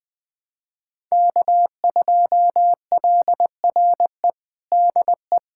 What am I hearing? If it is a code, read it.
Morse code, heard as a keyed tone, about 20 wpm, 700 Hz. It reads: K2LRE DE
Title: USA Callsign DE Callsign - 5 Characters - 20wpm